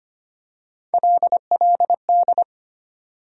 Morse code, heard as a keyed tone, about 25 wpm, 700 Hz. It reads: LLD
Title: USA Call Sign Postfixes - 3 Letters 25wpm